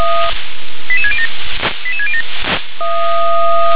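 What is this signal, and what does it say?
Five-tone selective call (ZVEI), heard as quick runs of five tones, tones of about 70 ms. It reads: 88607 88607
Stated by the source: Five-tone ZVEI selective call received by SDR.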